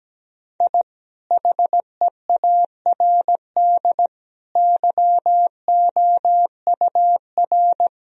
Morse code, heard as a keyed tone, about 17 wpm, 700 Hz. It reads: I HEARD YOUR
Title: Sentences from Top 1000 Words - Review Part 1 - 17wpm